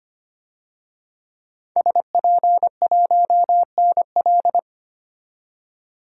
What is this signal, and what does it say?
Morse code, heard as a keyed tone, about 25 wpm, 700 Hz. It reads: SP1NL
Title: DXCC-100 Call Signs - 5 Characters - 25wpm